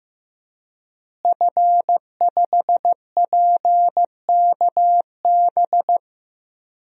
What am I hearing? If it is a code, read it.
Morse code, heard as a keyed tone, about 15 wpm, 700 Hz. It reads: F5PKB
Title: DXCC-100 Call Signs - 5 Characters - Repeated 3x - 15wpm